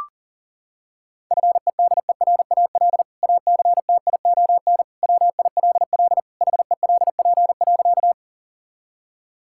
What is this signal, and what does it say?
Morse code, heard as a keyed tone, about 40 wpm, 700 Hz. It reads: FEDERAL ACTION WILL HELP.
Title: Sentences from Top 1000 Words - Review Part 5 - 40wpm